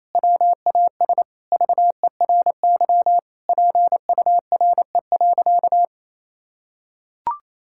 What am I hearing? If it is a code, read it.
Morse code, heard as a keyed tone, about 28 wpm, 700 Hz. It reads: WAS VERY PURE.
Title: Sentences from Top 1600 Words 28wpm